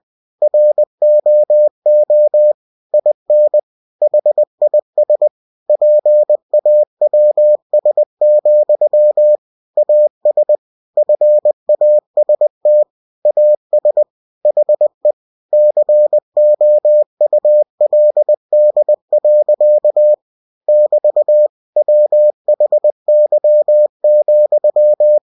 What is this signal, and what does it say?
Morse code, heard as a keyed tone, about 20 wpm, 600 Hz. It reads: ROO IN HIS PAWS, AS FAST AS HE COULD. <BT> WHY,
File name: cw-books / winnie-the-pooh / winnie-20wpm-page-0069